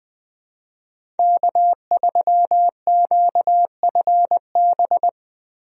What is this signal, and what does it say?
Morse code, heard as a keyed tone, about 20 wpm, 700 Hz. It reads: K3QFB
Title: USA Call Signs - 5 Characters - 20wpm